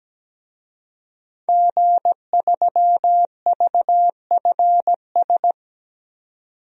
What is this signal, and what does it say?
Morse code, heard as a keyed tone, about 17 wpm, 700 Hz. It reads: G3VFS
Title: DXCC-100 Call Signs - 5 Characters - 17wpm